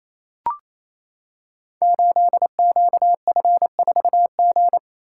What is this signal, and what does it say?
Morse code, heard as a keyed tone, about 27 wpm, 700 Hz. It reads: 8QF4G
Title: Code Groups - 5 Letter:Numbers - 28wpm